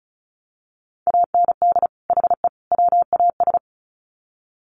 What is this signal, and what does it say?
Morse code, heard as a keyed tone, about 34 wpm, 700 Hz. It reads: AND HE WAS
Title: Sets of 3 Words 35wpm